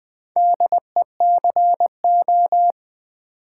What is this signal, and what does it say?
Morse code, heard as a keyed tone, about 21 wpm, 700 Hz. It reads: DECO
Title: Build A Word - 2 Letters At A Time - 8-Letter Words - 20wpm